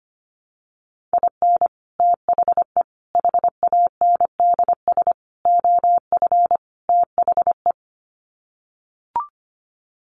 Morse code, heard as a keyed tone, about 25 wpm, 700 Hz. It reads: IN THE HANDS OF THE
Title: Sets of 5 Words 25wpm